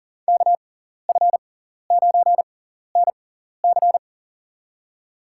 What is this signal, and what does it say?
Morse code, heard as a keyed tone, about 40 wpm, spaced as for 14 wpm, 700 Hz. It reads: KF9NC
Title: USA Call Signs - 5 Characters - 6x Character Spacing - 40wpm